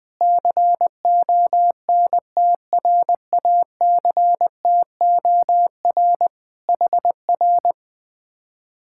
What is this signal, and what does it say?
Morse code, heard as a keyed tone, about 20 wpm, 700 Hz. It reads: CONTRACTOR HR